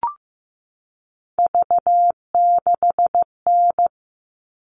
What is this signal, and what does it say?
Morse code, heard as a keyed tone, about 15 wpm, 700 Hz. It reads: V6N
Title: All Cut Numbers - 3-Digit 15wpm